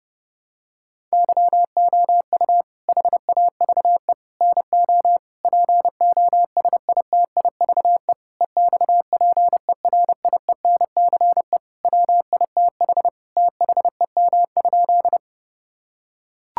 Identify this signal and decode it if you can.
Morse code, heard as a keyed tone, about 28 wpm, 700 Hz. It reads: YOU HAVE NO POSITIVE EXPERIENCE WITH THEM?
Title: Sentences from Top 1000 Words - Review Part 5 - 30wpm